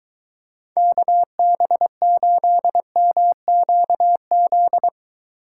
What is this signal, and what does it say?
Morse code, heard as a keyed tone, about 22 wpm, 700 Hz. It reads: KB8MQZ